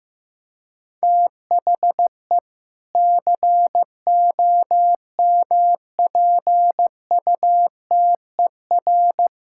Morse code, heard as a keyed tone, about 16 wpm, 700 Hz. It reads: THE COMPUTER
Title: Sentences from Top 1000 Words - Review Part 6 - 15wpm